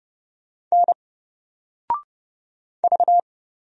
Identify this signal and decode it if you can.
Morse code, heard as a keyed tone, about 30 wpm, 700 Hz. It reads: N V
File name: Single-Letter-Number-30wpm